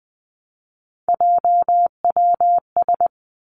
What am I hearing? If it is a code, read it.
Morse code, heard as a keyed tone, about 20 wpm, 700 Hz. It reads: JWS
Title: Three Challenging Letter-Number Combinations 20wpm